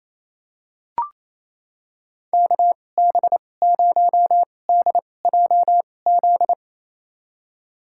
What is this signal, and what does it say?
Morse code, heard as a keyed tone, about 28 wpm, spaced as for 20 wpm, 700 Hz. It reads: KB0DJZ
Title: USA Call Signs - 2x Character Spacing - 28wpm